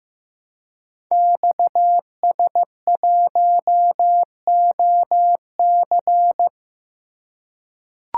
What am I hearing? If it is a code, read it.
Morse code, heard as a keyed tone, about 15 wpm, 700 Hz. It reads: XS1OC
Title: Code Groups - 5 Letter:Numbers - 15wpm